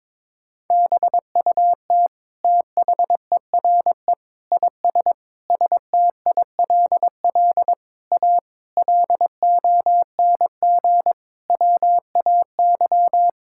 Morse code, heard as a keyed tone, about 22 wpm, 700 Hz. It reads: BUT THERE IS STILL A LONG WAY
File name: Sentences with 9-Word Phrases - 22wpm